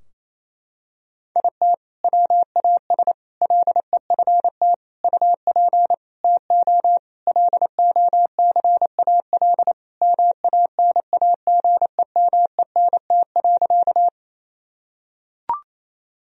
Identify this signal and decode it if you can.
Morse code, heard as a keyed tone, about 28 wpm, 700 Hz. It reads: IT WAS LEFT UP TO LOCAL MANAGEMENT.
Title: Sentences from Top 1000 Words - Review Part 4 - 28wpm